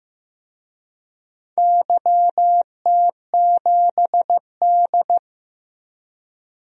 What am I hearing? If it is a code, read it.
Morse code, heard as a keyed tone, about 15 wpm, 700 Hz. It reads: YT7D
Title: DXCC-100 Call Signs - 4 Characters - 15wpm